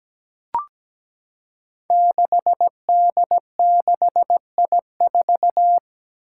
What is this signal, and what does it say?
Morse code, heard as a keyed tone, about 17 wpm, 700 Hz. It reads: 6D6I4
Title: Five Challenging Letter-Number Combinations 17wpm